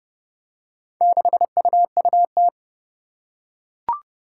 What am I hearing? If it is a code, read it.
Morse code, heard as a keyed tone, about 30 wpm, 700 Hz. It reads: 6UUT
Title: Four Challenging Letter-Number Combinations 30wpm